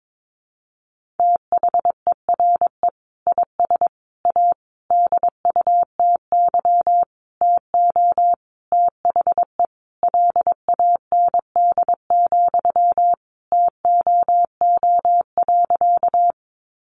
Morse code, heard as a keyed tone, about 22 wpm, 700 Hz. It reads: THERE IS A DUTY TO THE LAND, TOO.
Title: Sentences from Top 1500 Words 22wpm